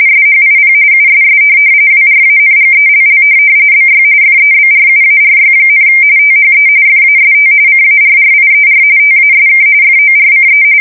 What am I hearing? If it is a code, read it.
RTTY, about 45.45 baud. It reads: SHIP USS WRANGELL BY FEBRUARY, THE FAST CARRIER TASK FORCE
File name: fsk170 / USS-Massachusetts